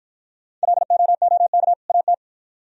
Morse code, heard as a keyed tone, about 50 wpm, 700 Hz. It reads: LOOK AT